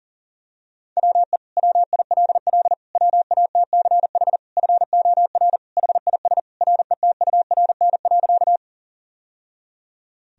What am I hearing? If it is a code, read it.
Morse code, heard as a keyed tone, about 40 wpm, 700 Hz. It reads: WE WILL WATCH FOR HIS RETURN.
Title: Sentences from Top 1000 Words - Review Part 1 - 40wpm